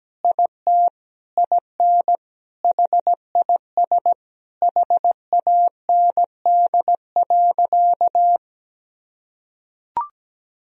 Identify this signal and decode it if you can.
Morse code, heard as a keyed tone, about 17 wpm, 700 Hz. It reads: IT IN HIS HAND.
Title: Sentences from Top 1000 Words - Review Part 2 - 17wpm